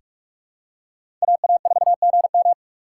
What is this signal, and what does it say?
Morse code, heard as a keyed tone, about 47 wpm, 700 Hz. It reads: AA4GM
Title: USA Call Signs - 5 Characters - 6x Character Spacing - 45wpm